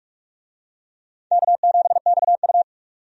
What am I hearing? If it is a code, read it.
Morse code, heard as a keyed tone, about 47 wpm, 700 Hz. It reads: K7XU
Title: USA Call Signs - 4 Characters - 45wpm